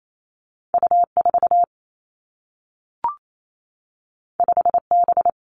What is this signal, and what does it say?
Morse code, heard as a keyed tone, about 28 wpm, 700 Hz. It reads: U 4 5 B
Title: Two Challenging Letter-Number Combinations 28wpm